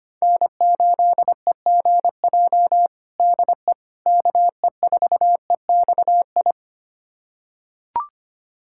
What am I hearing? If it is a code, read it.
Morse code, heard as a keyed tone, about 25 wpm, 700 Hz. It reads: N8EGJ DE KE4EXI
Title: USA Callsign DE Callsign 25wpm